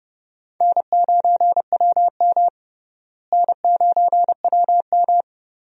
Morse code, heard as a keyed tone, about 29 wpm, 700 Hz. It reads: N9WM N9WM
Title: USA Call Signs - 4 Characters - Repeated 2x - 30wpm